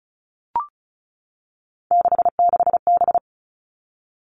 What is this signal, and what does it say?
Morse code, heard as a keyed tone, about 34 wpm, 700 Hz. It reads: 66B
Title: All Cut Numbers - 3-Digit 35wpm